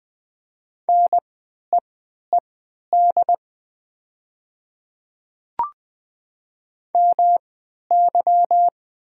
Morse code, heard as a keyed tone, about 20 wpm, spaced as for 11 wpm, 700 Hz. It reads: NEED MY
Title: 3x Character Spacing - Top 300 Words - 20wpm